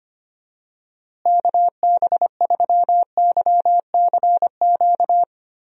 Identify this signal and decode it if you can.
Morse code, heard as a keyed tone, about 25 wpm, 700 Hz. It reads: KB3YCQ
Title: USA Call Signs - 6 Characters - Repeated 3x - 25wpm